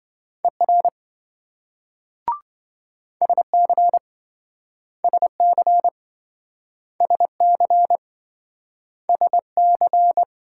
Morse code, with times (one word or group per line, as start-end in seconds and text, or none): 0.4-0.9 ER
3.2-4.0 SC
5.0-5.9 SC
7.0-8.0 SC
9.1-10.2 SC